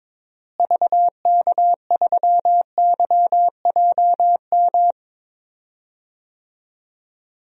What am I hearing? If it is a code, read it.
Morse code, heard as a keyed tone, about 21 wpm, 700 Hz. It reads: VK3YJM
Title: DXCC-100 Call Signs - 6 Characters - Repeated 2x - 22wpm